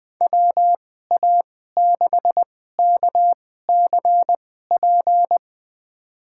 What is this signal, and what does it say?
Morse code, heard as a keyed tone, about 21 wpm, spaced as for 15 wpm, 700 Hz. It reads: WA6KCP
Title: USA Call Signs - 6 Characters - 2x Character Spacing - 20wpm